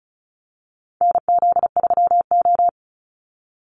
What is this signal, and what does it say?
Morse code, heard as a keyed tone, about 34 wpm, 700 Hz. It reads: NZ3O
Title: Call Signs Repeated 3x - 35wpm